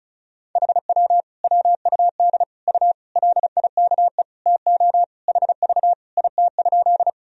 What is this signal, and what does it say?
Morse code, heard as a keyed tone, about 34 wpm, 700 Hz. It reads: HW WUD U LIKE TO HV IT?
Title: Sentences from Top 100 Words and Abbreviations 35wpm